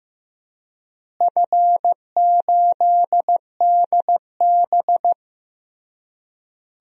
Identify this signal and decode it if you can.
Morse code, heard as a keyed tone, about 15 wpm, 700 Hz. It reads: F8DB
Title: DXCC-100 Call Signs - 4 Characters - Repeated 3x - 15wpm